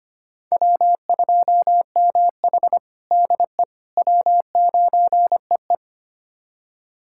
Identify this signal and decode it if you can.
Morse code, heard as a keyed tone, about 25 wpm, 700 Hz. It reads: W2MH DE W9EE
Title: USA Callsign DE Callsign - 4 Characters - 25wpm